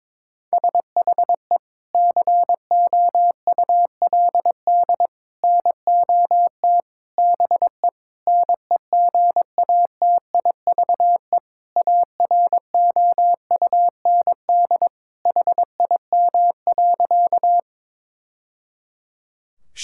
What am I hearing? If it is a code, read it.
Morse code, heard as a keyed tone, about 22 wpm, 700 Hz. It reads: SHE COULD NOT BE NEGATIVE AROUND HIM.
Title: Sentences from Top 1000 Words - Review Part 6 - 22wpm